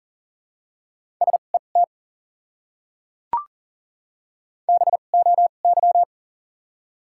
Morse code, heard as a keyed tone, about 40 wpm, spaced as for 29 wpm, 700 Hz. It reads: SET BOY
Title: 2x Character Spacing - Top 200 Words - 40wpm